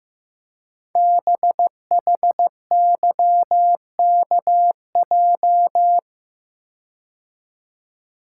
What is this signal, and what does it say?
Morse code, heard as a keyed tone, about 15 wpm, 700 Hz. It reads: BHYKJ
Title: Code Groups - 5 Letters - 15wpm